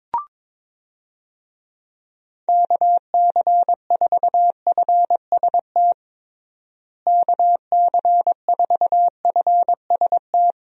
Morse code, heard as a keyed tone, about 22 wpm, 700 Hz. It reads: KC4FST KC4FST
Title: USA Call Signs - 6 Characters - Repeated 2x - 22wpm